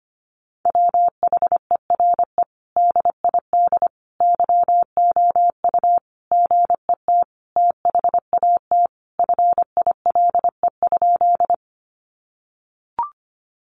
Morse code, heard as a keyed tone, about 25 wpm, 700 Hz. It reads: WHERE DID YOU GET THAT FILE?
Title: Sentences from Top 1200 Words 25wpm